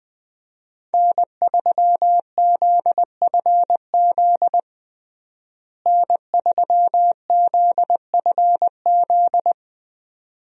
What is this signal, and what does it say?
Morse code, heard as a keyed tone, about 20 wpm, 700 Hz. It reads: N3ZFZ N3ZFZ
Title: USA Call Signs - 5 Characters - Repeated 2x - 20wpm